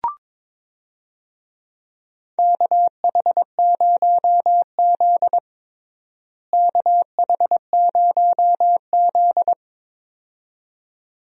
Morse code, heard as a keyed tone, about 22 wpm, 700 Hz. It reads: KH0Z KH0Z
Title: USA Call Signs - 4 Characters - Repeated 2x - 22wpm